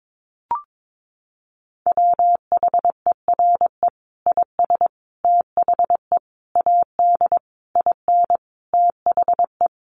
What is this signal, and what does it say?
Morse code, heard as a keyed tone, about 22 wpm, 700 Hz. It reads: WHERE IS THE AD IN THE
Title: Sentences from Top 1300 Words 22wpm